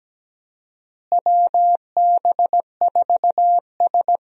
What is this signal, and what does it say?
Morse code, heard as a keyed tone, about 17 wpm, 700 Hz. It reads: WB4S
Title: USA Call Signs - 4 Characters - Repeated 2x - 17wpm